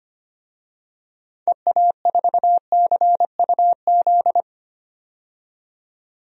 Morse code, heard as a keyed tone, about 25 wpm, 700 Hz. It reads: EA4CUZ
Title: DXCC-100 Call Signs - 6 Characters - 25wpm